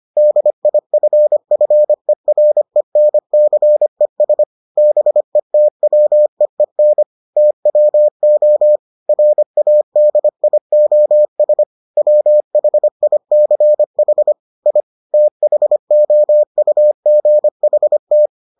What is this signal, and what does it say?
Morse code, heard as a keyed tone, about 25 wpm, 600 Hz. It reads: DIFFERENCES BETWEEN TWO RADIOS WHICH I THOUGHT